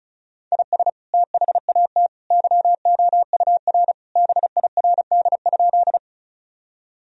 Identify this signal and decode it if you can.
Morse code, heard as a keyed tone, about 36 wpm, 700 Hz. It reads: IS THAT YOUR BIRD?
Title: Sentences from Top 1000 Words - Review Part 2 - 35wpm